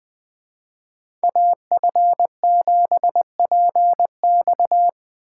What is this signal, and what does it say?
Morse code, heard as a keyed tone, about 19 wpm, 700 Hz. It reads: AF7PX